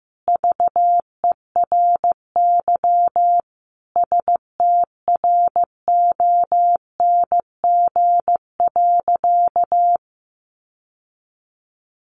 Morse code, heard as a keyed tone, about 14 wpm, 700 Hz. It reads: VERY STRONG.
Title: Sentences from Top 1300 Words 15wpm